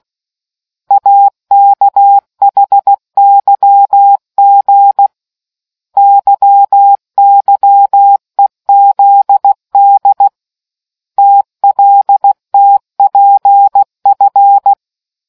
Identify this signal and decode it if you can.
Morse code, heard as a keyed tone, about 16 wpm, 800 Hz. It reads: AKHYG YYEZD TLTPF